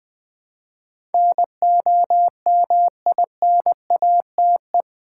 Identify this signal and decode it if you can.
Morse code, heard as a keyed tone, about 20 wpm, 700 Hz. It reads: NOMINATE
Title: Build A Word - 2 Letters At A Time - 8-Letter Words - 20wpm